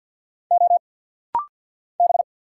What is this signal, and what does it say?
Morse code, heard as a keyed tone, about 50 wpm, 700 Hz. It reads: O B
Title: Single Letters - Rapid-Fire 50wpm